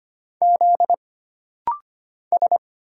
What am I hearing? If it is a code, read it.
Morse code, heard as a keyed tone, about 25 wpm, 700 Hz. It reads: Z S
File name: Single Character (Letter:Number:Punctuation:Prosign) - Rapid-Fire - 25wpm